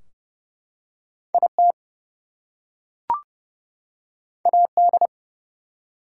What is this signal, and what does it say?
Morse code, heard as a keyed tone, about 30 wpm, 700 Hz. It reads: IT AD
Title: Two Letter Words - 30wpm